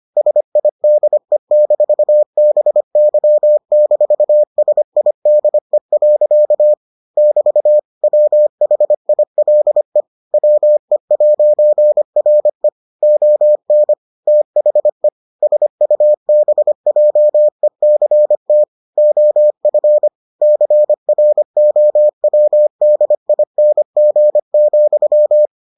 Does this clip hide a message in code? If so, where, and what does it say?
Yes: Morse code from 0.2 to 25.5 s, heard as a keyed tone, about 25 wpm, 600 Hz. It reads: SIDE-BY-SIDE. <BT> WHILE WE'RE ON THE SUBJECT OF CROWDING,